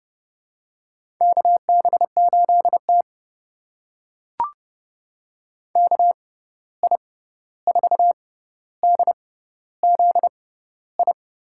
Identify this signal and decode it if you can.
Morse code, heard as a keyed tone, about 30 wpm, spaced as for 14 wpm, 700 Hz. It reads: KB8T KI4DZI
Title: USA Call Signs - 6x Character Spacing - 30wpm